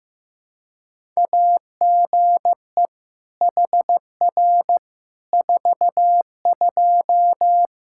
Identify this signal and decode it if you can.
Morse code, heard as a keyed tone, about 16 wpm, 700 Hz. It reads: AGE HR 42
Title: Practice QSO - Exchange 3 - More Info - 15wpm